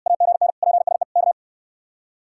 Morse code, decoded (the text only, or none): AGN PSE K